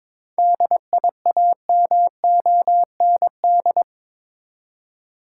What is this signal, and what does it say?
Morse code, heard as a keyed tone, about 22 wpm, 700 Hz. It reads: DIAMOND
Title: Seven Letter Words - 22wpm